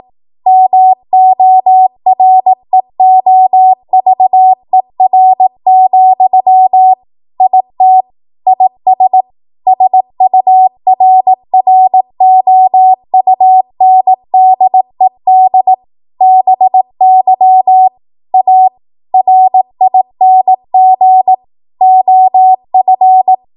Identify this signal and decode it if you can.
Morse code, heard as a keyed tone, about 18 wpm, 750 Hz. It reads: MOREOVER, IT IS SURROUNDED BY A RING OF
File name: Morse / 18WPM